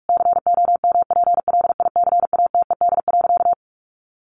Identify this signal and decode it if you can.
Morse code, heard as a keyed tone, about 44 wpm, 700 Hz. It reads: COMPLICATED.
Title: Sentences from Top 1200 Words 45wpm